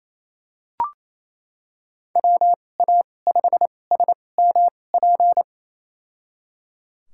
Morse code, heard as a keyed tone, about 28 wpm, spaced as for 20 wpm, 700 Hz. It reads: WA5SMP
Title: USA Call Signs - 2x Character Spacing - 28wpm